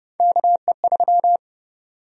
Morse code, heard as a keyed tone, about 30 wpm, 700 Hz. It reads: KE3
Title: USA Call Sign Prefixes - 30wpm